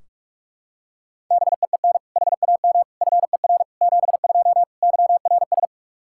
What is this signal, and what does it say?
Morse code, heard as a keyed tone, about 43 wpm, 700 Hz. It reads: BEEN HAM FER 72 YRS
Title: Practice QSO - Exchange 3 - More Info - 45wpm